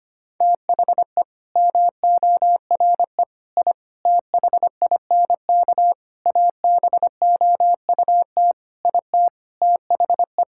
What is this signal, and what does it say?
Morse code, heard as a keyed tone, about 25 wpm, 700 Hz. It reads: THE MORE I THINK ABOUT IT THE
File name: Sets of 9 Words - 25wpm